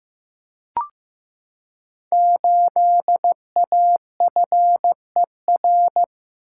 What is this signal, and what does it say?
Morse code, heard as a keyed tone, about 15 wpm, 700 Hz. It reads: 8AFER
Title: Code Groups - 5 Letter:Numbers - 15wpm